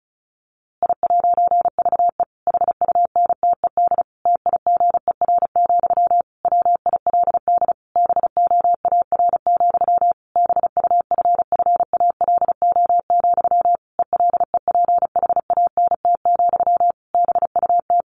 Morse code, heard as a keyed tone, about 34 wpm, 700 Hz. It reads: I'VE HUNTED TIGER, WILD BOAR, BUFFALO, ELEPHANT, BUT
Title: A Sound Of Thunder 35wpm